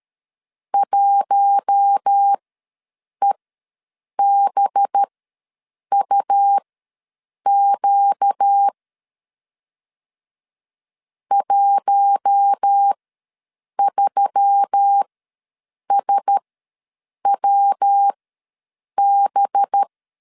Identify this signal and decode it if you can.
Morse code, heard as a keyed tone, about 13 wpm, spaced as for 7 wpm, 800 Hz. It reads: 1EBUQ 13SWB
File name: LEZIONI / 5 LBVC / 5 LB ws 12wpm 3034